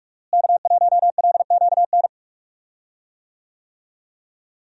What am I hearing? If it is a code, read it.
Morse code, heard as a keyed tone, about 46 wpm, 700 Hz. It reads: K1LQN
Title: USA Call Signs - 5 Characters - Repeated 2x - 45wpm